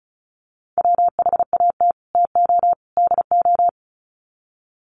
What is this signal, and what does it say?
Morse code, heard as a keyed tone, about 34 wpm, 700 Hz. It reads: WHAT TO DO
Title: Sets of 3 Words 35wpm